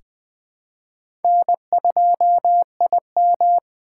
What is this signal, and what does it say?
Morse code, heard as a keyed tone, about 21 wpm, 700 Hz. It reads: N2IM
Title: USA Call Signs - 4x Character Spacing - 20wpm